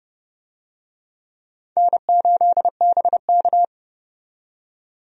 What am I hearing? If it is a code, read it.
Morse code, heard as a keyed tone, about 29 wpm, 700 Hz. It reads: N8BK
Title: USA Call Signs - 4 Characters - 30wpm